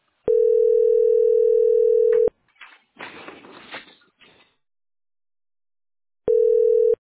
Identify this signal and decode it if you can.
Ringback tone (North America), 440 + 480 Hz.